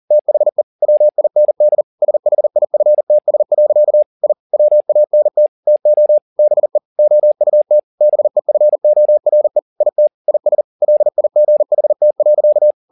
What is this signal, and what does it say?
Morse code, heard as a keyed tone, about 40 wpm, 600 Hz. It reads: THE WIND SHIFTS. I WANT TO BE OUT BEFORE IT IS LIGHT.